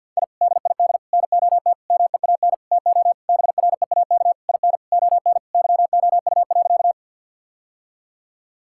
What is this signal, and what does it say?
Morse code, heard as a keyed tone, about 50 wpm, 700 Hz. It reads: I DID NOT MEAN TO BREAK IN ON YOU.
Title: Sentences from Top 1000 Words - Review Part 6 - 50wpm